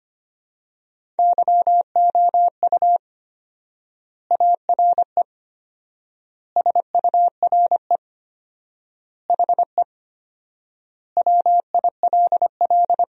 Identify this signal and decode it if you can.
Morse code, heard as a keyed tone, about 25 wpm, 700 Hz. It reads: YOU ARE SURE HE WILL
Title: 4x Word-Spacing - Sentences from Top 300 Words - 25wpm